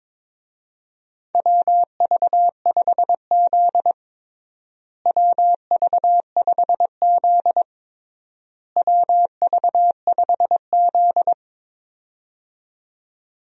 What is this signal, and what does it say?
Morse code, heard as a keyed tone, about 22 wpm, 700 Hz. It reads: WV5Z WV5Z WV5Z
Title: USA Call Signs - 4 Characters - Repeated 3x - 22wpm